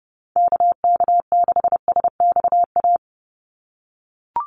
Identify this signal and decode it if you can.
Morse code, heard as a keyed tone, about 30 wpm, 700 Hz. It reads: KK6SXA
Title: USA Call Signs 30wpm